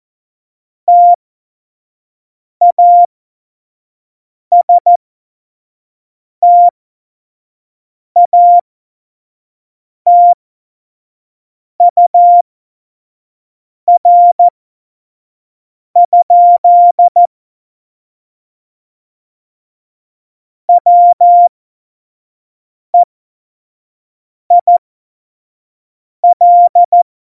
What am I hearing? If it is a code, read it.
Morse code, heard as a keyed tone, about 14 wpm, spaced as for 5 wpm, 700 Hz. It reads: TASTATUR? WEIL
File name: Morsekurs / Witze / Witz 18 Juli 17, Tempo 25 F70